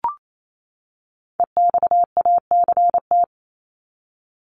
Morse code, heard as a keyed tone, about 28 wpm, 700 Hz. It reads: EXACT
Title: Top 1251-1500 Words 28wpm